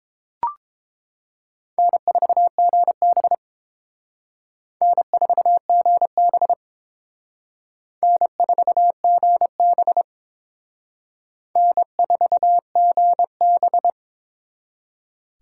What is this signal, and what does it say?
Morse code, heard as a keyed tone, about 30 wpm, 700 Hz. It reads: N4GB N4GB N4GB N4GB